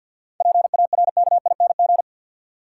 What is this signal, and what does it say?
Morse code, heard as a keyed tone, about 53 wpm, 700 Hz. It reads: PARKING
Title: Seven Letter Words - 50wpm